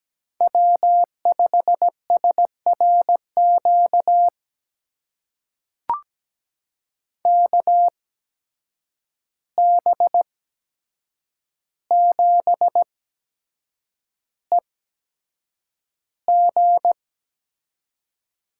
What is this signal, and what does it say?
Morse code, heard as a keyed tone, about 17 wpm, spaced as for 7 wpm, 700 Hz. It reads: W5SRQ KB7EG
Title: USA Call Signs - 8x Character Spacing - 17wpm